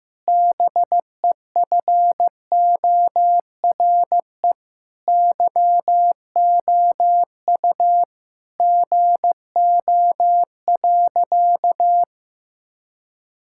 Morse code, heard as a keyed tone, about 15 wpm, 700 Hz. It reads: BEFORE YOU GO.
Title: Sentences from Top 1000 Words - Review Part 6 - 15wpm